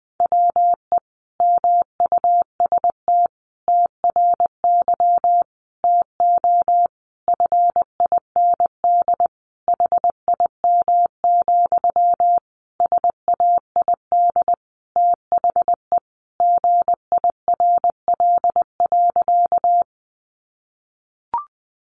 Morse code, heard as a keyed tone, about 20 wpm, 700 Hz. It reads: WE MUST TRY TO FIND HIM, SAID THE GIRL.
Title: Sentences from Top 400 Words 20wpm